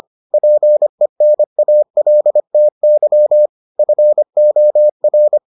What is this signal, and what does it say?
Morse code, heard as a keyed tone, about 25 wpm, 600 Hz. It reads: PENALTY FOR